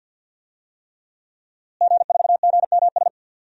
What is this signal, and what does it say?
Morse code, heard as a keyed tone, about 51 wpm, 700 Hz. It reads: G4GMS
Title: DXCC-100 Call Signs - 5 Characters - Repeated 3x - 50wpm